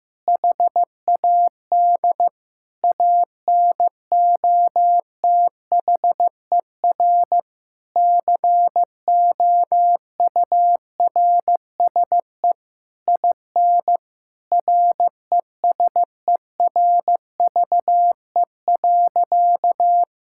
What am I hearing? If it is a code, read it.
Morse code, heard as a keyed tone, about 15 wpm, 700 Hz. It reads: HAD ANOTHER COURSE IN RESERVE.
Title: Sentences from Top 1000 Words - Review Part 6 - 15wpm